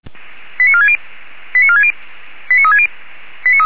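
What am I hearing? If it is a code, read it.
Five-tone selective call (ZVEI), heard as quick runs of five tones, tones of about 70 ms. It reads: 89460 89460 89360
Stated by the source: ZVEI five-tone selective call received on an SDR.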